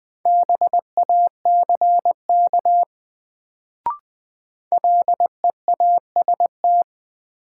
Morse code, heard as a keyed tone, about 20 wpm, 700 Hz. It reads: BACK LEAST
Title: Sets of 2 Words - New Word Drill - 20wpm